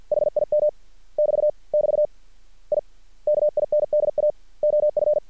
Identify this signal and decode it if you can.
Morse code, heard as a keyed tone, about 50 wpm, 600 Hz. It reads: HIM <BT> <BT> I KINDA OF
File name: archived / CRSnet